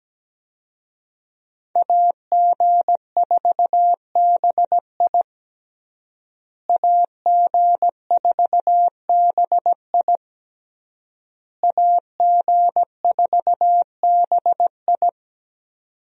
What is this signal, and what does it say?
Morse code, heard as a keyed tone, about 17 wpm, 700 Hz. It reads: AG4BI AG4BI AG4BI
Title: USA Call Signs - 5 Characters - Repeated 3x - 17wpm